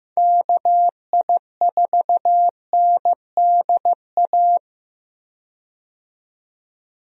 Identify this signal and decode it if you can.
Morse code, heard as a keyed tone, about 15 wpm, 700 Hz. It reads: KI4NDA
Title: USA Call Signs - 6 Characters - 15wpm